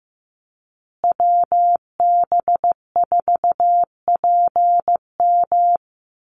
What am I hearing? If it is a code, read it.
Morse code, heard as a keyed tone, about 15 wpm, 700 Hz. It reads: WB4PM
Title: USA Call Signs Repeated 2x 15wpm